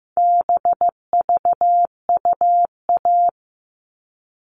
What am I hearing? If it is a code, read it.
Morse code, heard as a keyed tone, about 15 wpm, 700 Hz. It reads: BVUA
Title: All Cut Numbers - 4-Digit 15wpm